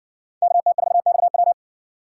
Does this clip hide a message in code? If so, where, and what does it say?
Yes: Morse code from 0.4 to 1.5 s, heard as a keyed tone, about 57 wpm, 700 Hz. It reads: KT4XW